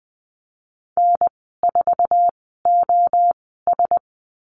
Morse code, heard as a keyed tone, about 20 wpm, spaced as for 15 wpm, 700 Hz. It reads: N4OS
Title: 048 Lesson 9g - Number 4 - Call Signs 20wpm